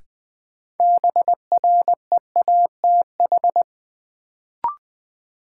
Morse code, heard as a keyed tone, about 20 wpm, 700 Hz. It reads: BREATH
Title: Sets of 5 Words Encore - New Word Drill - 20wpm